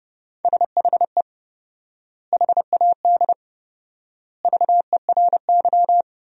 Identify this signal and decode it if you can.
Morse code, heard as a keyed tone, about 30 wpm, 700 Hz. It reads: SHE HAD VERY
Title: 4x Word-Spacing - Sentences from Top 400 Words - 30wpm